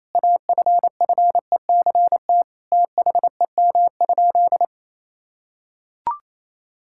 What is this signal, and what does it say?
Morse code, heard as a keyed tone, about 28 wpm, 700 Hz. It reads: AFFECT THEM?
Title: Sentences from Top 1000 Words - Review Part 6 - 28wpm